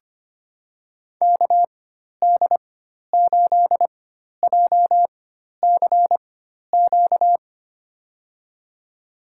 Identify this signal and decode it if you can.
Morse code, heard as a keyed tone, about 25 wpm, spaced as for 12 wpm, 700 Hz. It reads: KD8JCQ
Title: USA Call Signs - 4x Character Spacing - 25wpm